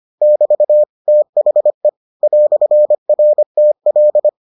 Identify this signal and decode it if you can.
Morse code, heard as a keyed tone, about 25 wpm, 600 Hz. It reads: <BT> THE "RTL